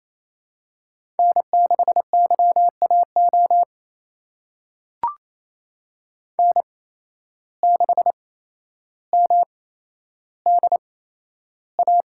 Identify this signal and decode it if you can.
Morse code, heard as a keyed tone, about 28 wpm, spaced as for 12 wpm, 700 Hz. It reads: N6YAO N6MDA
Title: USA Call Signs - 5 Characters - 8x Character Spacing - 28wpm